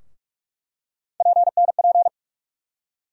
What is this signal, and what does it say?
Morse code, heard as a keyed tone, about 44 wpm, 700 Hz. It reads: PNP
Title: USA Call Sign Postfixes - 3 Letters 45wpm